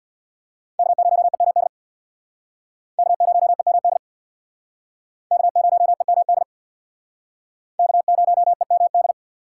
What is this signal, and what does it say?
Morse code, heard as a keyed tone, about 62 wpm, 700 Hz. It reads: K0EMD K0EMD K0EMD K0EMD
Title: Speed Racing - USA 1x3 Call Signs 50wpm